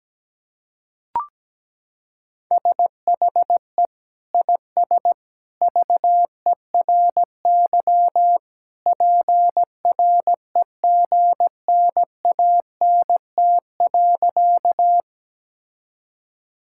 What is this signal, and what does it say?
Morse code, heard as a keyed tone, about 17 wpm, 700 Hz. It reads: SHE IS VERY PREGNANT.
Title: Sentences from Top 1000 Words - Review Part 6 - 17wpm